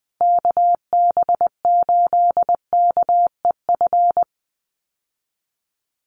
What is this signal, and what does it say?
Morse code, heard as a keyed tone, about 20 wpm, 700 Hz. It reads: KB8KEF
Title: USA Call Signs Repeated 3x 20wpm